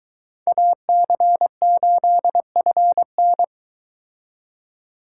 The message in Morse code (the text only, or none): AC8FN